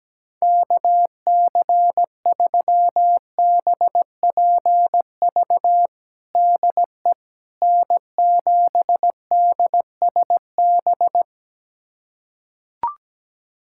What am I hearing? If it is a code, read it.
Morse code, heard as a keyed tone, about 17 wpm, 700 Hz. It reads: KC3BPV DE N7DSB
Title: USA Callsign DE Callsign 17wpm